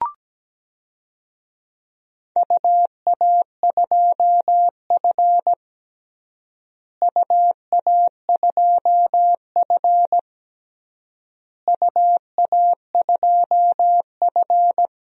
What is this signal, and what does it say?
Morse code, heard as a keyed tone, about 17 wpm, 700 Hz. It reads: UA2F UA2F UA2F
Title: DXCC-100 Call Signs - 4 Characters - Repeated 3x - 17wpm